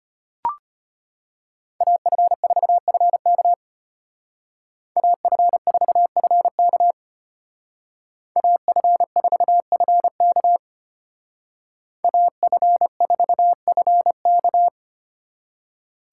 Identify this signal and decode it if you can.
Morse code, heard as a keyed tone, about 31 wpm, 700 Hz. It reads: AF4FK AF4FK AF4FK AF4FK
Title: Speed Racing - USA 2x2 Call Signs 25wpm